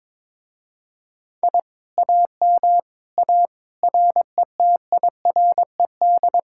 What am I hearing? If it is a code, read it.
Morse code, heard as a keyed tone, about 22 wpm, 700 Hz. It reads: I AM A RETIRED
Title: Speed-Racing Common QSO Phrases - 22wpm